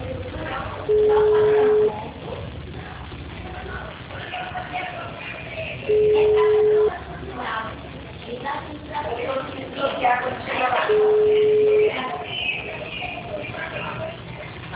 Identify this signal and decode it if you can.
Ringback tone (Europe, 425 Hz), with speech.